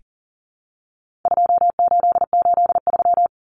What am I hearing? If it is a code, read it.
Morse code, heard as a keyed tone, about 40 wpm, 700 Hz. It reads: 2883
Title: Common Cut Numbers - 4-Digit 40wpm